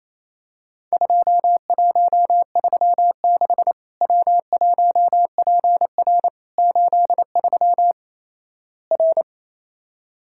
Morse code, heard as a keyed tone, about 28 wpm, 700 Hz. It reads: 2136 W1PR 83 R
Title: Worked All Europe DX Contest - QTC Exchange - 28wpm